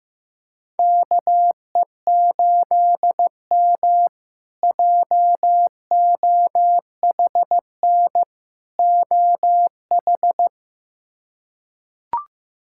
Morse code, heard as a keyed tone, about 15 wpm, 700 Hz. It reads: KE8M JOHN OH
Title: K1USN SST QSO Practice - 15wpm